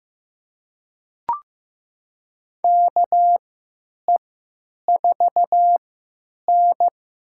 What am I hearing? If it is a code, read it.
Morse code, heard as a keyed tone, about 15 wpm, spaced as for 9 wpm, 700 Hz. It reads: KE4N
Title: USA Call Signs - 6 Characters - 3x Character Spacing - 15wpm